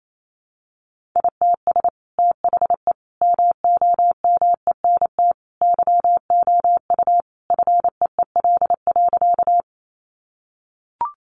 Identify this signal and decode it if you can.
Morse code, heard as a keyed tone, about 28 wpm, 700 Hz. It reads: ITS THE MOMENT YOU FEEL.
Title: Sentences from Top 1000 Words 28wpm